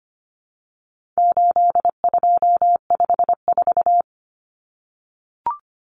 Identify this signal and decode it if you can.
Morse code, heard as a keyed tone, about 25 wpm, 700 Hz. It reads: 8254
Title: Common Cut Numbers - 4-Digit 25wpm